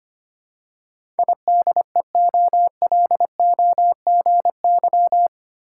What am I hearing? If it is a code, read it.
Morse code, heard as a keyed tone, about 25 wpm, 700 Hz. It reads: IDEOLOGY
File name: 8 Letter Words - First 3 Letters Given - 25wpm